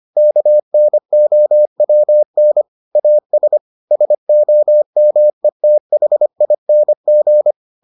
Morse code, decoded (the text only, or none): KNOWN AS SOMETHING